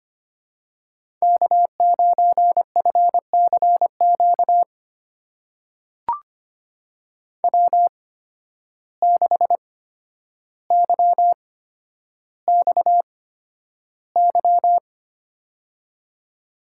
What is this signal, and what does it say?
Morse code, heard as a keyed tone, about 25 wpm, spaced as for 11 wpm, 700 Hz. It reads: K9FCQ W6YXY
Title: USA Call Signs - 5 Characters - 8x Character Spacing - 25wpm